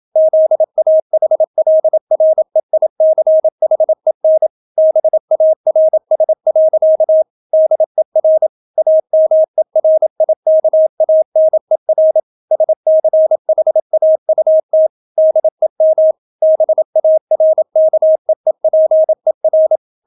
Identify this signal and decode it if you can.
Morse code, heard as a keyed tone, about 27 wpm, 630 Hz. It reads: ZAHLREICHEN BARS. DER AMERIKANER SCHAUT DEM BARKEEPER